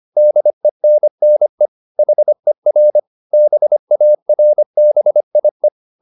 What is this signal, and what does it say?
Morse code, heard as a keyed tone, about 25 wpm, 600 Hz. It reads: DENNE HER BARBIE
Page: Ferie 25 wpm